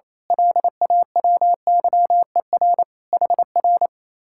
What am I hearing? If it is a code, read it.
Morse code, heard as a keyed tone, about 28 wpm, 700 Hz. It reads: LAWYER HR
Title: Speed-Racing Common QSO Phrases - 28wpm